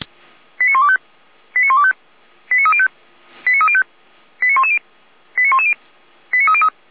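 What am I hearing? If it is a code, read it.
Five-tone selective call (ZVEI), heard as quick runs of five tones, tones of about 70 ms. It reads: 89125 89125 89305 89305 89110 89110 89303
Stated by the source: ZVEI five-tone selective call received on an SDR, from the feuerwehr-orlamuende fire brigade server.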